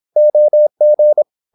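Morse code, heard as a keyed tone, about 26 wpm, 600 Hz. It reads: OG
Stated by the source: High-speed (QRQ) Morse at 26 wpm.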